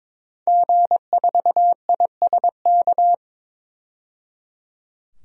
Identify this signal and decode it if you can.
Morse code, heard as a keyed tone, about 22 wpm, 700 Hz. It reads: G4ISK
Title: DXCC-100 Call Signs - 5 Characters - 22wpm